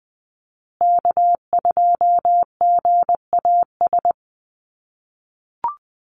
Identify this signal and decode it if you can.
Morse code, heard as a keyed tone, about 20 wpm, 700 Hz. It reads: K2GAS
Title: USA Call Signs Repeated 2x 20wpm